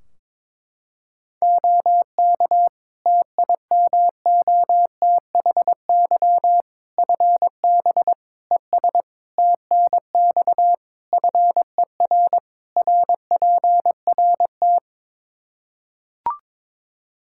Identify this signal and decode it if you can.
Morse code, heard as a keyed tone, about 21 wpm, 700 Hz. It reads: OK TIMOTHY FB ES TNX FER RPRT
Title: Practice QSO - Exchange 2 - Info - 22wpm